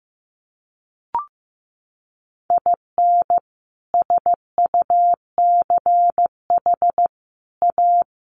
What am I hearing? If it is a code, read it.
Morse code, heard as a keyed tone, about 15 wpm, 700 Hz. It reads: IN SUCH A
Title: Sets of 5 Words Encore 15wpm